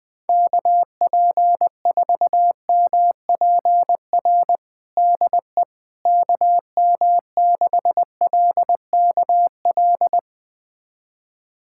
Morse code, heard as a keyed tone, about 20 wpm, 700 Hz. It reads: KP4MPR DE KM6LKL
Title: USA Callsign DE Callsign - 6 Characters - 20wpm